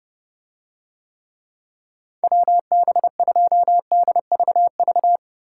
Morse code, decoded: WB2DVV

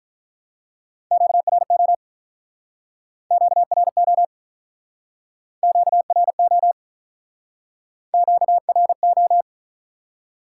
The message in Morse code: QRO QRO QRO QRO

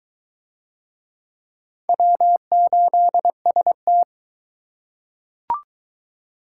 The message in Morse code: W8ST

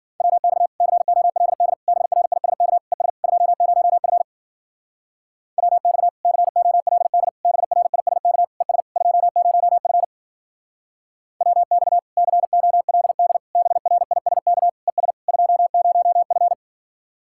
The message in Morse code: WX COLD BRISK ES 20F WX COLD BRISK ES 20F WX COLD BRISK ES 20F

